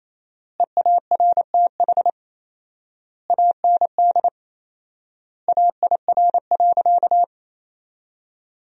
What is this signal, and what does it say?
Morse code, heard as a keyed tone, about 28 wpm, 700 Hz. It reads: EARTH AND AIR.
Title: 4x Word-Spacing - Sentences from Top 400 Words - 28wpm